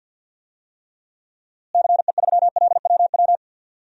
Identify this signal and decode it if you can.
Morse code, heard as a keyed tone, about 52 wpm, 700 Hz. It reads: CE3LWW
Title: DXCC-100 Call Signs - 6 Characters - Repeated 3x - 50wpm